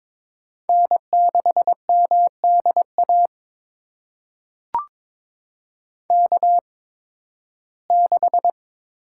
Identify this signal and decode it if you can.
Morse code, heard as a keyed tone, about 22 wpm, spaced as for 14 wpm, 700 Hz. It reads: N6MDA K6
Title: USA Call Signs - 5 Characters - 8x Character Spacing - 22wpm